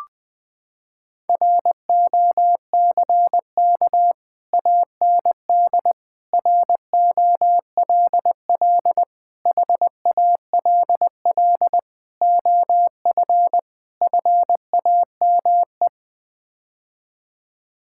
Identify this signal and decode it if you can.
Morse code, heard as a keyed tone, about 20 wpm, 700 Hz. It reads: ROCK AND ROLL HALL OF FAME
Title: Sets of 6 Words - 20wpm